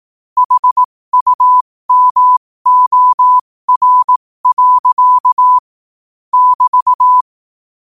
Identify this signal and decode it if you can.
Morse code, heard as a keyed tone, about 18 wpm, spaced as for 15 wpm, 1000 Hz. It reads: HUMOR. <BT>
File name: qotdfiles / 15wpm